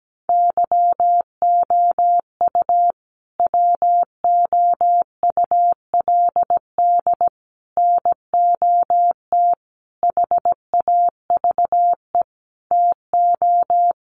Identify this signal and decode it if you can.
Morse code, heard as a keyed tone, about 17 wpm, 700 Hz. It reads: YOU WOULD NOT HAVE TO
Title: Sets of 5 Words Encore 17wpm